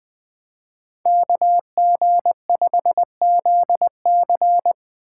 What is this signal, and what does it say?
Morse code, heard as a keyed tone, about 20 wpm, 700 Hz. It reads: KG5ZC
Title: Speed Racing - USA 2x2 Call Signs 17wpm